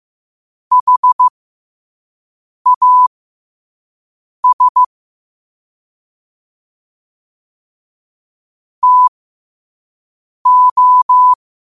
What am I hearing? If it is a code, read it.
Morse code, heard as a keyed tone, about 15 wpm, spaced as for 5 wpm, 1000 Hz. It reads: HAS TO
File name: qotdfiles / 5wpm